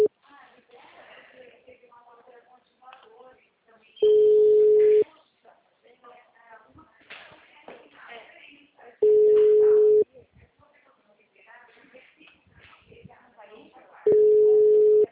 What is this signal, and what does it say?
Ringback tone (Europe, 425 Hz), with speech.
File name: out- -2017- - - .60942